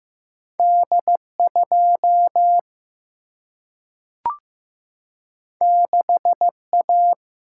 Morse code, heard as a keyed tone, about 15 wpm, 700 Hz. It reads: D 2 6 A
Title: Single-Letter Single-Number 15wpm